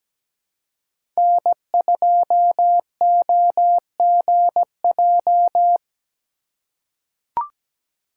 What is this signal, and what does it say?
Morse code, heard as a keyed tone, about 17 wpm, 700 Hz. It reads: N2OGJ
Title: USA Call Signs - 2x Character Spacing - 17wpm